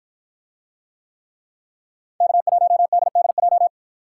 Morse code, heard as a keyed tone, about 51 wpm, 700 Hz. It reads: K1DDJ